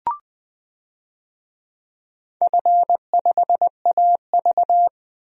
Morse code, heard as a keyed tone, about 21 wpm, 700 Hz. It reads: F5AV
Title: DXCC-100 Call Signs - 4 Characters - Repeated 2x - 20wpm